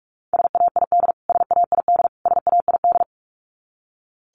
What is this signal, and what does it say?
Morse code, heard as a keyed tone, about 44 wpm, 700 Hz. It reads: SAID SAID SAID
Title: Top 500 Words Repeated 45wpm